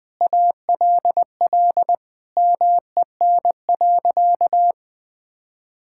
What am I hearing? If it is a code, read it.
Morse code, heard as a keyed tone, about 21 wpm, 700 Hz. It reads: ALL MEN.
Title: Sentences from Top 1000 Words - Review Part 6 - 20wpm